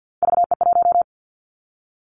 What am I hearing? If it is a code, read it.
Morse code, heard as a keyed tone, about 50 wpm, 700 Hz. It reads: VE1
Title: Callsign Prefixes for Top 100 DXCC - 50wpm